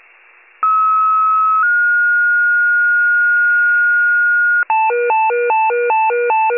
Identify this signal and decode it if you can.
Two-tone page, heard as a one-second tone, then a three-second tone, about 1290 then 1430 Hz. It is heard from 0.6 to 4.6 s.